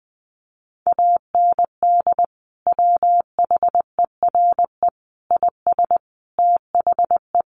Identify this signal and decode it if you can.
Morse code, heard as a keyed tone, about 20 wpm, 700 Hz. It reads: AND WHERE IS THE
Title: Sentences from Top 1100 Words 20wpm